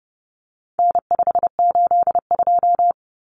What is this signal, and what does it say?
Morse code, heard as a keyed tone, about 30 wpm, 700 Hz. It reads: N582
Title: Common Cut Numbers - 4-Digit 30wpm